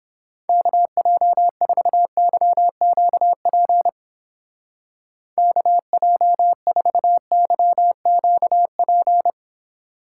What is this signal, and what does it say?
Morse code, heard as a keyed tone, about 31 wpm, 700 Hz. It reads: KJ4YQP KJ4YQP